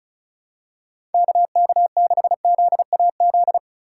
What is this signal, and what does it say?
Morse code, heard as a keyed tone, about 34 wpm, 700 Hz. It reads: KK6ZAZ
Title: USA Call Signs - 6 Characters - 3x Character Spacing - 35wpm